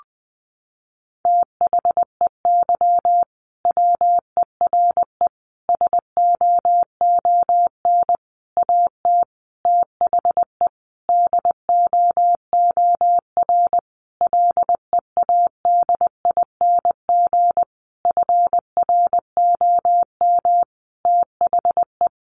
Morse code, heard as a keyed tone, about 20 wpm, 700 Hz. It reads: THEY WERE SOON AT THE DOOR LEADING FROM THE
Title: Sentences from Top 1200 Words 20wpm